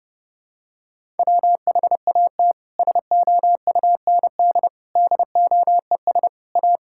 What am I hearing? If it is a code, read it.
Morse code, heard as a keyed tone, about 30 wpm, 700 Hz. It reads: WHAT SOUND DOES A
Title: Sentences from Top 1000 Words - Review Part 5 - 30wpm